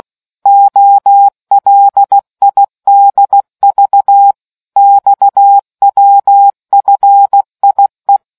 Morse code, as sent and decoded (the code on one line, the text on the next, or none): --- .-.. .. -.. ...- / -..- .-- ..-. .. .
OLIDV XWFIE